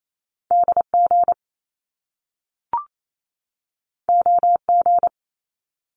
Morse code, heard as a keyed tone, about 28 wpm, 700 Hz. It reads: DG OG
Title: Top Two Letter Combinations in Words 28wpm